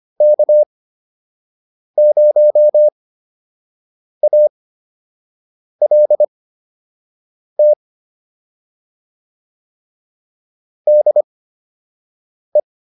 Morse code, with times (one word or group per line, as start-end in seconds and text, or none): 0.2-7.7 K0ALT
10.9-12.6 DE